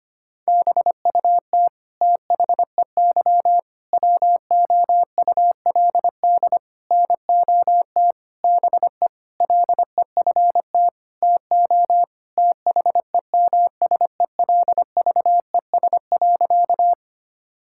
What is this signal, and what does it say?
Morse code, heard as a keyed tone, about 25 wpm, 700 Hz. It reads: BUT THEY WOULD NOT BE LEFT TO THEMSELVES.
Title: Sentences from Top 1000 Words - Review Part 5 - 25wpm